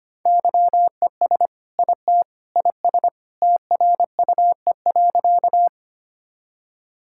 Morse code, heard as a keyed tone, about 25 wpm, 700 Hz. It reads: YES IT IS TRUE.
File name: practice / 25wpm / Sentences with 2-Word Phrases - 25wpm